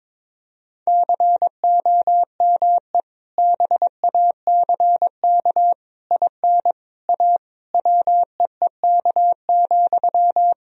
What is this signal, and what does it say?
Morse code, heard as a keyed tone, about 22 wpm, 700 Hz. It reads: COME BACK IN A WEEK,
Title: Sentences from Top 1000 Words - Review Part 3 - 22wpm